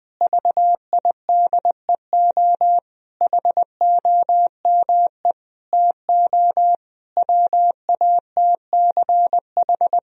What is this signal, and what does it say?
Morse code, heard as a keyed tone, about 21 wpm, 700 Hz. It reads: VIDEO HOME TO WATCH
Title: Sentences from Top 1000 Words - Review Part 4 - 20wpm